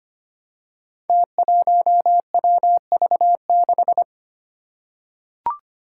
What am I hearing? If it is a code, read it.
Morse code, heard as a keyed tone, about 25 wpm, 700 Hz. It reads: T1WV6
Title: Five Challenging Letter-Number Combinations 25wpm